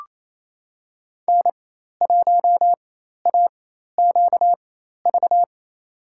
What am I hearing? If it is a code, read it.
Morse code, heard as a keyed tone, about 28 wpm, spaced as for 13 wpm, 700 Hz. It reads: N1AQV
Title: USA Call Signs - 4x Character Spacing - 28wpm